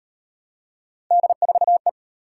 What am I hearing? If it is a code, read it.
Morse code, heard as a keyed tone, about 35 wpm, 700 Hz. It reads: D4E